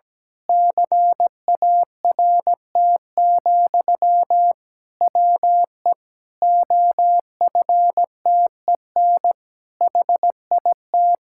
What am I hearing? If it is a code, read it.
Morse code, heard as a keyed tone, about 17 wpm, 700 Hz. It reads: CART, WE OFTEN HIT
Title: Pangram Challenge v2 - 17wpm